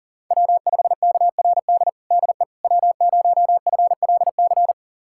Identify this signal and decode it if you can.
Morse code, heard as a keyed tone, about 40 wpm, 700 Hz. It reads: W5KRD DE W0FLC
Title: USA Callsign DE Callsign - 5 Characters - 40wpm